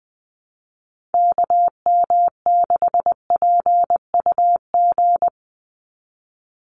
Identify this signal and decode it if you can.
Morse code, heard as a keyed tone, about 20 wpm, 700 Hz. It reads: KM6PUG
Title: USA Call Signs 20wpm